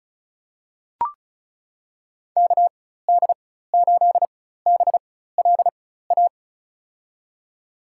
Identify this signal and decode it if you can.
Morse code, heard as a keyed tone, about 36 wpm, spaced as for 16 wpm, 700 Hz. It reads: KD8BLA
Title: USA Call Signs - 6 Characters - 4x Character Spacing - 35wpm